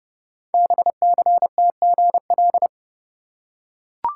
Morse code, heard as a keyed tone, about 29 wpm, 700 Hz. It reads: BCTGL
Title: Code Groups - 5 Letters - 30wpm